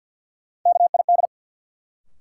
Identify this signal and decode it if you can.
Morse code, heard as a keyed tone, about 50 wpm, 700 Hz. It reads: KID
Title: Sentences from Top 1600 Words - New Word Drill - 50wpm